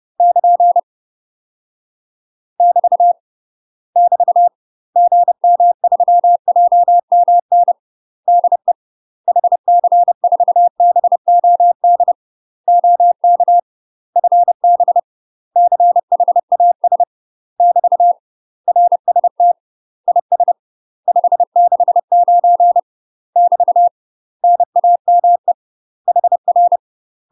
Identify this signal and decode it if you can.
Morse code, heard as a keyed tone, about 30 wpm, 700 Hz. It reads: <KN> <BT> <BT> GM3JMN DE HC4BOD OK FB CHAS <BT> RST IS 569 <BT> NAME HR
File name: UK MASTER QSOs TRAINING / 30WPM / UK MASTER QSOs 0051